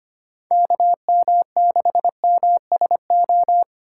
Morse code, heard as a keyed tone, about 25 wpm, 700 Hz. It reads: KM6MSO
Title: USA Call Signs - 6 Characters - 3x Character Spacing - 25wpm